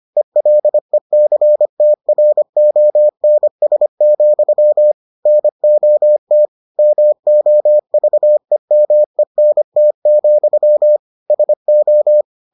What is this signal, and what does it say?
Morse code, heard as a keyed tone, about 25 wpm, 600 Hz. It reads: ELECTRONS, NOT MOVEMENT, SO